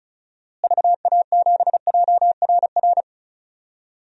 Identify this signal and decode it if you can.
Morse code, heard as a keyed tone, about 36 wpm, 700 Hz. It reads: VA7JRR
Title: DXCC-100 Call Signs - 6 Characters - Repeated 3x - 35wpm